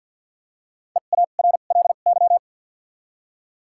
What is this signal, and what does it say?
Morse code, heard as a keyed tone, about 53 wpm, spaced as for 37 wpm, 700 Hz. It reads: EARLY
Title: 2x Character Spacing - Top 500 Words - 50wpm